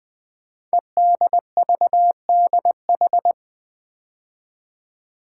Morse code, heard as a keyed tone, about 20 wpm, 700 Hz. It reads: EDVDH
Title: Code Groups - 5 Letters - 20wpm